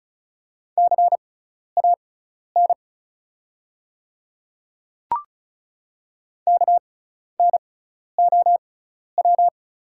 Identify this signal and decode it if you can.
Morse code, heard as a keyed tone, about 34 wpm, spaced as for 12 wpm, 700 Hz. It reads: CAN KNOW
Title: 6x Character Spacing - Top 100 Words - 35wpm